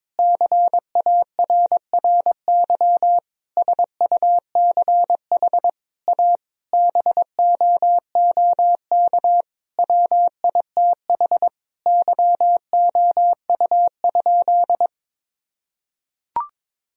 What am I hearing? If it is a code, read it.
Morse code, heard as a keyed tone, about 22 wpm, 700 Hz. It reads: CARRY SUCH A BOOK WITH YOU?
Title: Sentences from Top 1000 Words - Review Part 1 - 22wpm